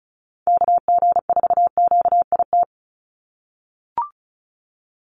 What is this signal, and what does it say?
Morse code, heard as a keyed tone, about 34 wpm, 700 Hz. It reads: KG4QIT
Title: USA Call Signs 35wpm